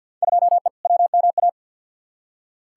Morse code, heard as a keyed tone, about 48 wpm, 700 Hz. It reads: 2E WMA
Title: Field Day Exchanges Encore - 50wpm